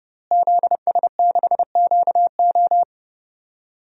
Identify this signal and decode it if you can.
Morse code, heard as a keyed tone, about 30 wpm, 700 Hz. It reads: ZS6QO
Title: DXCC-100 Call Signs - 5 Characters - Repeated 3x - 30wpm